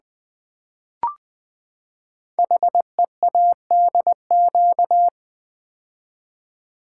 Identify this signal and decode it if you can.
Morse code, heard as a keyed tone, about 20 wpm, 700 Hz. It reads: HEADQ
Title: Build A Word - 5 Letters At A Time - 12-Letter Words - 20wpm